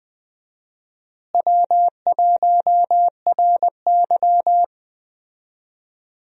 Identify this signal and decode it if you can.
Morse code, heard as a keyed tone, about 20 wpm, 700 Hz. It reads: W1RY
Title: USA Call Signs - 4 Characters - Repeated 3x - 20wpm